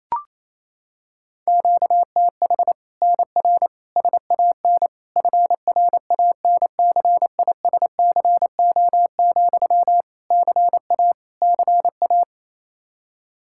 Morse code, heard as a keyed tone, about 28 wpm, 700 Hz. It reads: QTH NR SAN FRANCISCO, CA CA
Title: Practice QSO - Exchange 1 - Beginning - 28wpm